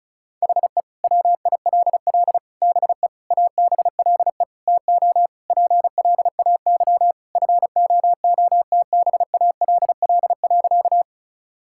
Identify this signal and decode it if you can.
Morse code, heard as a keyed tone, about 34 wpm, 700 Hz. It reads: HE WILL BE ABLE TO PLAY FOOTBALL.
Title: Sentences from Top 1600 Words 35wpm